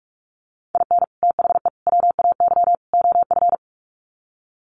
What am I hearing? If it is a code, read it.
Morse code, heard as a keyed tone, about 44 wpm, 700 Hz. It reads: IN THE WAY OF
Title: Sets of 4 Words 45wpm